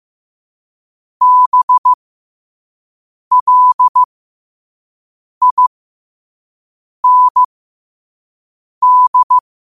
Morse code, heard as a keyed tone, about 15 wpm, spaced as for 5 wpm, 1000 Hz. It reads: BLIND